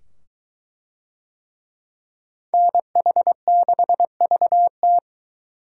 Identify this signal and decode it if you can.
Morse code, heard as a keyed tone, about 21 wpm, 700 Hz. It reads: NH6VT